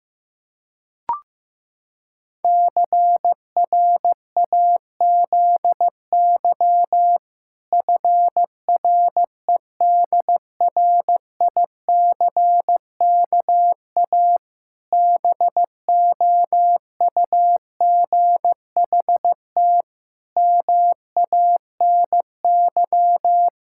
Morse code, heard as a keyed tone, about 15 wpm, 700 Hz. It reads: CRAZY FREDRICKA BOUGHT MANY
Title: Pangram Challenge v2 - 15wpm